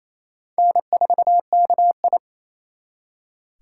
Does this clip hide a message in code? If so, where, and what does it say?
Yes: Morse code from 0.6 to 2.2 s, heard as a keyed tone, about 35 wpm, 700 Hz. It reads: N4KI